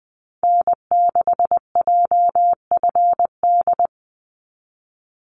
Morse code, heard as a keyed tone, about 20 wpm, 700 Hz. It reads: N6JFD
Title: NA SOTA Activator Callsigns 20wpm